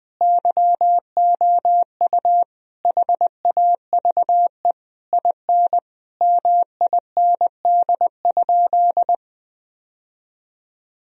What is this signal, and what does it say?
Morse code, heard as a keyed tone, about 20 wpm, 700 Hz. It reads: YOU HAVE IN MIND?
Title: Sentences from Top 1000 Words - Review Part 4 - 20wpm